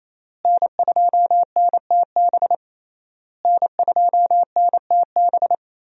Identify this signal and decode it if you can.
Morse code, heard as a keyed tone, about 28 wpm, 700 Hz. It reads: N2NTB N2NTB
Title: USA Call Signs - 5 Characters - Repeated 3x - 28wpm